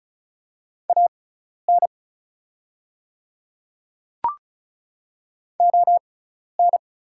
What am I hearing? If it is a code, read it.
Morse code, heard as a keyed tone, about 36 wpm, spaced as for 12 wpm, 690 Hz. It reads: AN ON
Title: 6x Character Spacing - Top 200 Words - 35wpm